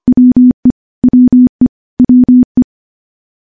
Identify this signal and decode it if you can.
Morse code, heard as a keyed tone, about 25 wpm, 250 Hz. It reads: WE WE WE
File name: morse / morse-training-25WPM